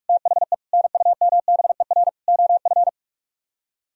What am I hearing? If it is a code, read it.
Morse code, heard as a keyed tone, about 49 wpm, 700 Hz. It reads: THE NUMBER OF